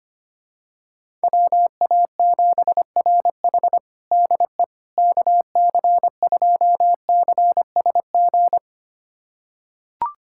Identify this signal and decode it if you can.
Morse code, heard as a keyed tone, about 25 wpm, 700 Hz. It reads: WA7RH DE KC2CSG
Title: USA Callsign DE Callsign 25wpm